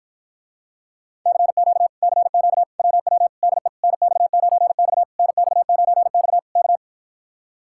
Morse code, heard as a keyed tone, about 53 wpm, 700 Hz. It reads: CQ CQ WW DE NX9X NX9X K